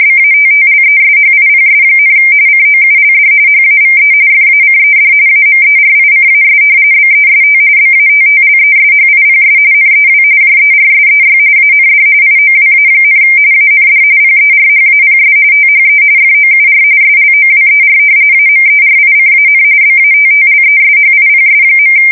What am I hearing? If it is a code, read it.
RTTY, about 45.45 baud. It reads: BATTLESHIPS, COULD HAVE FACED AMERICAS BEST BATTLESHIPS NEW JERSEY AND IOWA, AND PERHAPS THE FOUR SLOWER AMERICAN BATTLESHIPS.
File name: fsk170 / battleshipnj